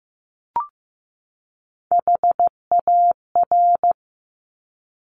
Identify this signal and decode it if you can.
Morse code, heard as a keyed tone, about 15 wpm, 700 Hz. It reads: HAR
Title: Top Three Letter Combinations in Words 15wpm